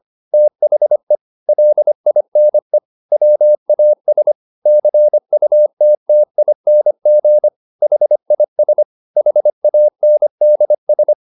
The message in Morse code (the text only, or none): THE LINE WAS CUTTING HIS HANDS